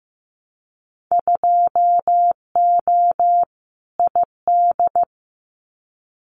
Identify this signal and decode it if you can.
Morse code, heard as a keyed tone, about 15 wpm, 700 Hz. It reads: 2O ID
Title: Winter Field Day Exchanges 15wpm